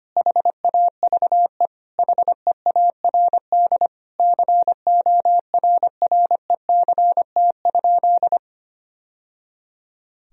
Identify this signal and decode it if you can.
Morse code, heard as a keyed tone, about 25 wpm, 700 Hz. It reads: HAVE HEARD CORRECT?
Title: Sentences from Top 1000 Words - Review Part 2 - 25wpm